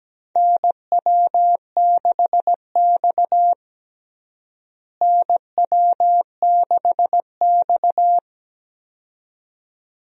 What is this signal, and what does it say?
Morse code, heard as a keyed tone, about 17 wpm, 700 Hz. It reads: NW6X NW6X
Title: USA Call Signs - 4 Characters - Repeated 2x - 17wpm